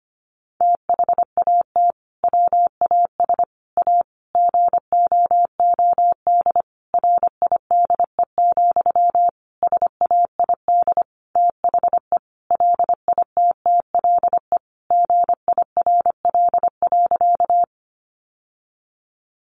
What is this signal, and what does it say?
Morse code, heard as a keyed tone, about 25 wpm, 700 Hz. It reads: THAT WAS A GOOD RIDE, SAID THE LITTLE GIRL.
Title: Sentences from Top 1200 Words 25wpm